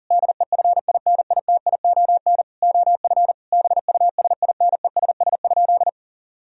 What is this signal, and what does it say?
Morse code, heard as a keyed tone, about 40 wpm, 700 Hz. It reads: DEFINITION OF BUSINESS?
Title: Sentences from Top 1000 Words - Review Part 4 - 40wpm